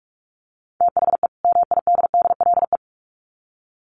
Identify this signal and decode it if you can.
Morse code, heard as a keyed tone, about 44 wpm, 700 Hz. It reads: THE MIDDLE
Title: Sets of 2 Words 45wpm